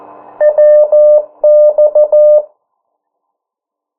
Morse code, heard as a keyed tone, about 14 wpm, 600 Hz. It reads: WX
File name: morse mp3 / 09 urok 8